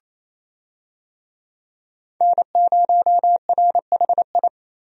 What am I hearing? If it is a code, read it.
Morse code, heard as a keyed tone, about 27 wpm, 700 Hz. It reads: N0RHI
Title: USA Call Signs - 5 Characters - 28wpm